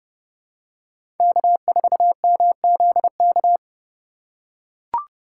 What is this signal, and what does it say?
Morse code, heard as a keyed tone, about 30 wpm, 700 Hz. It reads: K4MZK
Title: USA Call Signs - 5 Characters - 4x Character Spacing - 30wpm